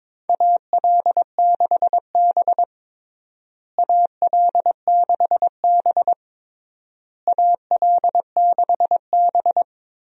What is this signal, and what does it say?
Morse code, heard as a keyed tone, about 22 wpm, 700 Hz. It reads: AL6B AL6B AL6B
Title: USA Call Signs - 4 Characters - Repeated 3x - 22wpm